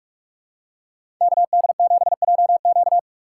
Morse code, heard as a keyed tone, about 46 wpm, 700 Hz. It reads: KD7JQ
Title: USA Call Signs - 5 Characters - 6x Character Spacing - 45wpm